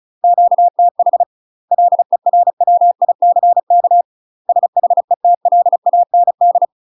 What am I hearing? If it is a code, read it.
Morse code, heard as a keyed tone, about 35 wpm, 700 Hz. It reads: QTH LERWICK SHETLAND
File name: UK MASTER QSOs TRAINING / 35WPM / UK MASTER QSOs 0021